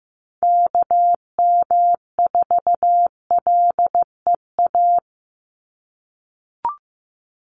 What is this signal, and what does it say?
Morse code, heard as a keyed tone, about 15 wpm, 700 Hz. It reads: KM4LEA
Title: USA Call Signs 15wpm